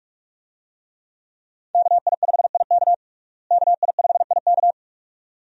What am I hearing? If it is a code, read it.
Morse code, heard as a keyed tone, about 46 wpm, 700 Hz. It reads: KI5IK KI5IK
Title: USA Call Signs - 5 Characters - Repeated 2x - 45wpm